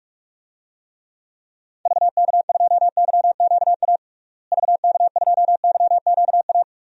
Morse code, heard as a keyed tone, about 42 wpm, 700 Hz. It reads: VK2YQA VK2YQA
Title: DXCC-100 Call Signs - 6 Characters - Repeated 2x - 45wpm